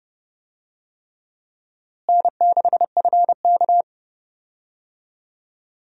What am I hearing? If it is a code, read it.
Morse code, heard as a keyed tone, about 30 wpm, 700 Hz. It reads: N6FK
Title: USA Call Signs - 4 Characters - Repeated 3x - 30wpm